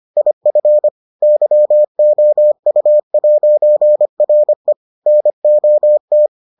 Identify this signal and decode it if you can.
Morse code, heard as a keyed tone, about 25 wpm, 600 Hz. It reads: IF YOU'RE NOT